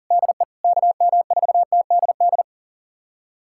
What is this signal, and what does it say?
Morse code, heard as a keyed tone, about 40 wpm, 700 Hz. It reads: DE KM4TDD
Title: USA Callsign DE Callsign - 6 Characters - 40wpm